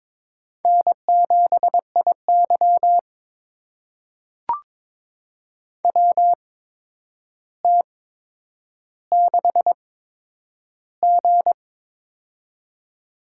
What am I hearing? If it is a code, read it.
Morse code, heard as a keyed tone, about 22 wpm, spaced as for 9 wpm, 700 Hz. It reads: N7IY WT6G
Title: USA Call Signs - 8x Character Spacing - 22wpm